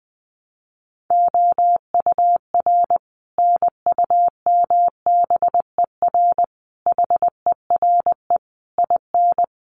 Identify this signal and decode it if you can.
Morse code, heard as a keyed tone, about 20 wpm, 700 Hz. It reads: OUR NUMBER HERE IN
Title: Sets of 5 Words Encore2 20wpm